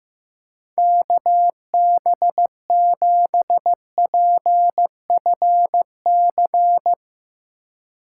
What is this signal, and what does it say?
Morse code, heard as a keyed tone, about 15 wpm, 700 Hz. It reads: KB7PFC
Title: USA Call Signs - 6 Characters - 2x Character Spacing - 15wpm